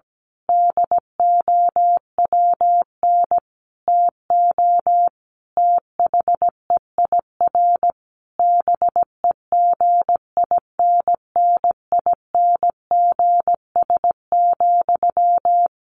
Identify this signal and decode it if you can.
Morse code, heard as a keyed tone, about 17 wpm, 700 Hz. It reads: DOWN TO THEIR BEGINNINGS,
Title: A Sound Of Thunder 17wpm